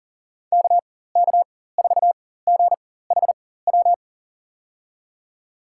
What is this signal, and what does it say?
Morse code, heard as a keyed tone, about 40 wpm, spaced as for 19 wpm, 700 Hz. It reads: KK4GHW
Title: USA Call Signs - 6 Characters - 4x Character Spacing - 40wpm